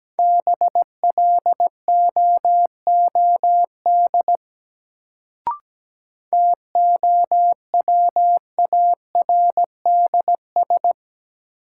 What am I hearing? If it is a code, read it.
Morse code, heard as a keyed tone, about 17 wpm, 700 Hz. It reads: BLOOD TOWARDS
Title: Sentences from Top 1000 Words - Review Part 2 - New Word Drill - 17wpm